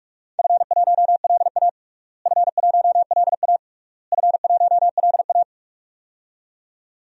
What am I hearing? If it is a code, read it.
Morse code, heard as a keyed tone, about 41 wpm, 700 Hz. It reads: F1LA F1LA F1LA
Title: DXCC-100 Call Signs - 4 Characters - Repeated 3x - 45wpm